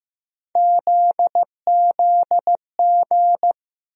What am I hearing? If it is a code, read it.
Morse code, heard as a keyed tone, about 16 wpm, 700 Hz. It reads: ZZG
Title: USA Call Sign Postfixes - 3 Letters 15wpm